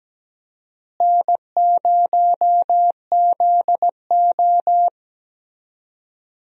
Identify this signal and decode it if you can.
Morse code, heard as a keyed tone, about 17 wpm, 700 Hz. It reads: N0ZO
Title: USA Call Signs - 4 Characters - Repeated 3x - 17wpm